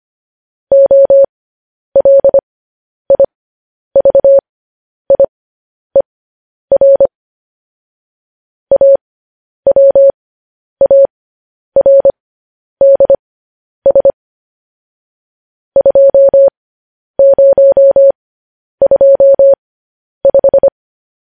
Morse code, heard as a keyed tone, about 25 wpm, spaced as for 10 wpm, 550 Hz. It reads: OLIVIER AWARDS 2025
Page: news headlines in morse code at 10 wpm